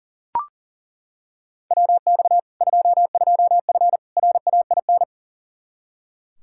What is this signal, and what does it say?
Morse code, heard as a keyed tone, about 40 wpm, 700 Hz. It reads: WX 22F RAIN
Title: Practice QSO - Exchange 2 - Info - 40wpm